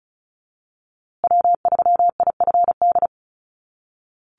Morse code, heard as a keyed tone, about 34 wpm, 700 Hz. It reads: W3IFD
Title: USA Call Signs Repeated 2x 35wpm